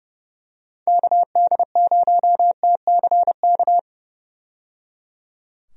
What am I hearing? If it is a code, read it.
Morse code, heard as a keyed tone, about 30 wpm, 700 Hz. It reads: KD0TCK
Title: USA Call Signs - 6 Characters - 30wpm